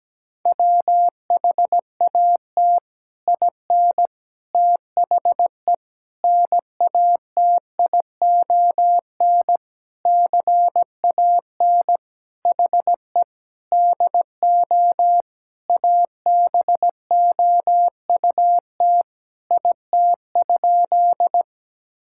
Morse code, heard as a keyed tone, about 17 wpm, 700 Hz. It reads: WHAT IN THE NATION CAN HE DO ABOUT IT?
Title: Sentences from Top 1000 Words - Review Part 6 - 17wpm